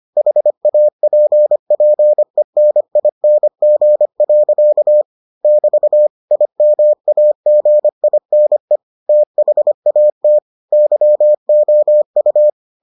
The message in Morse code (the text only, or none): HAPPENING. <BT> IMAGINE THAT YOU